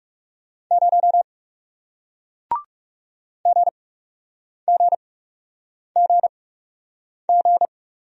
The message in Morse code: 0 GGGG

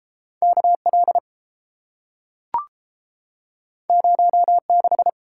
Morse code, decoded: KL 06

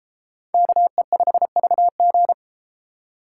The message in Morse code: KE5VG

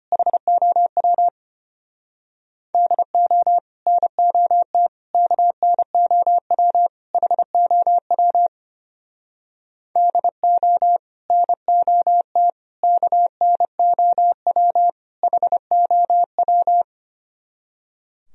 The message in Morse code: HOW DO NOT KNOW HOW DO NOT KNOW HOW

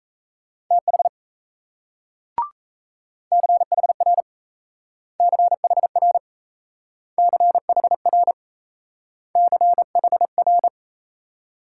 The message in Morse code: TH CHR CHR CHR CHR

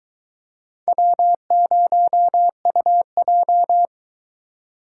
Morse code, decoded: W0UJ

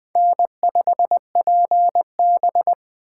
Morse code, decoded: N5PB

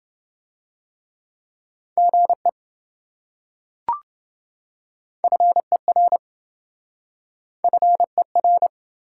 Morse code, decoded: GE FER FER

